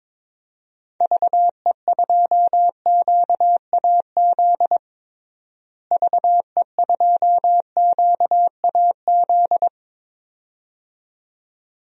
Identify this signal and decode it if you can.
Morse code, heard as a keyed tone, about 22 wpm, 700 Hz. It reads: VE2QAZ VE2QAZ
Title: DXCC-100 Call Signs - 6 Characters - Repeated 2x - 22wpm